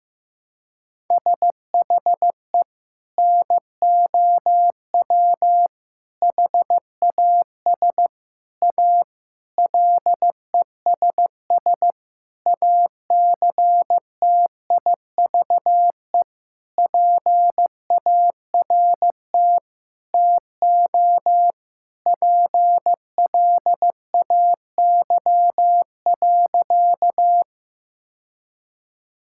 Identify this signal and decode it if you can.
Morse code, heard as a keyed tone, about 15 wpm, 700 Hz. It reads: SHE NOW HAS A LESS ACTIVE PART TO PLAY.
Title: Sentences from Top 1000 Words - Review Part 6 - 15wpm